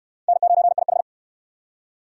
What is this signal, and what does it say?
Morse code, heard as a keyed tone, about 66 wpm, 700 Hz. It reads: N9IH